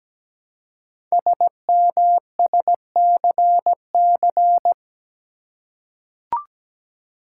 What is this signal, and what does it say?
Morse code, heard as a keyed tone, about 17 wpm, 700 Hz. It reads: SMSCC
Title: Code Groups - 5 Letters - 17wpm